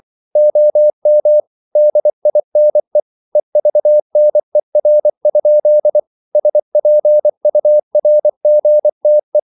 Morse code, heard as a keyed tone, about 24 wpm, 600 Hz. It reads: OM DINE EVNER? SPURGTE
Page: Ferie 24 wpm